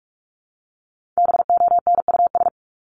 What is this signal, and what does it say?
Morse code, heard as a keyed tone, about 44 wpm, 700 Hz. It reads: BONUS
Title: Top 1751-2000 Words 45wpm